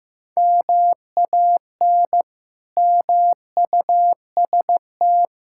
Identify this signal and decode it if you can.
Morse code, heard as a keyed tone, about 15 wpm, 700 Hz. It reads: MAN MUST
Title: Sentences from Top 1000 Words - Review Part 6 - 15wpm